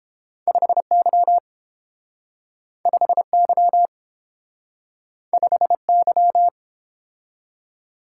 Morse code, heard as a keyed tone, about 32 wpm, 700 Hz. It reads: 5Y 5Y 5Y